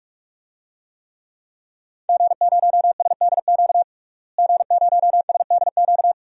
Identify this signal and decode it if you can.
Morse code, heard as a keyed tone, about 46 wpm, 700 Hz. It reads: G0SDQ G0SDQ
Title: DXCC-100 Call Signs - 5 Characters - Repeated 3x - 45wpm